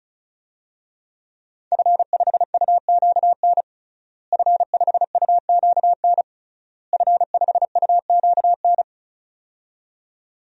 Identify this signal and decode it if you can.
Morse code, heard as a keyed tone, about 34 wpm, 700 Hz. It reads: F5UQN F5UQN F5UQN
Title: DXCC-100 Call Signs - 5 Characters - Repeated 3x - 35wpm